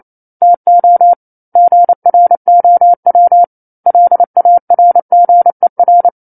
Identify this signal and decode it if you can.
Morse code, heard as a keyed tone, about 28 wpm, 700 Hz. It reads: TO GROW LARGER